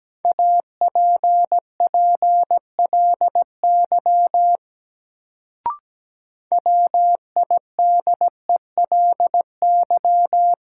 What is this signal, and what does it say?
Morse code, heard as a keyed tone, about 17 wpm, 700 Hz. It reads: APPLY WIDELY
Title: Sentences from Top 1200 Words - New Word Drill - 17wpm